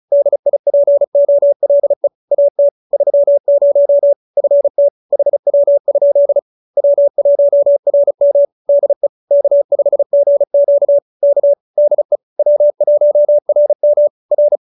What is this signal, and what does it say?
Morse code, heard as a keyed tone, about 34 wpm, 570 Hz. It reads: DIPOLE AT 30 FT HW? W1RM DE K5GQ K DE W1RM R